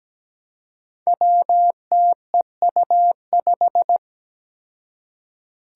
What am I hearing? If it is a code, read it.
Morse code, heard as a keyed tone, about 17 wpm, 700 Hz. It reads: WTEU5
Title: Five Challenging Letter-Number Combinations 17wpm